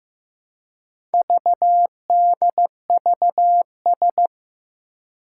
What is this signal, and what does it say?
Morse code, heard as a keyed tone, about 15 wpm, 700 Hz. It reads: VDVS
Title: Four Challenging Letter-Number Combinations 15wpm